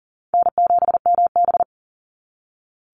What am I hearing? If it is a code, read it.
Morse code, heard as a keyed tone, about 40 wpm, 700 Hz. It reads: N7MB
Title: Call Signs - 40wpm